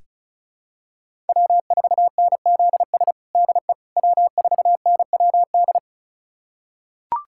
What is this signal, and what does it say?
Morse code, heard as a keyed tone, about 36 wpm, 700 Hz. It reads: W4NZS DE W4NWD
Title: USA Callsign DE Callsign - 5 Characters - 35wpm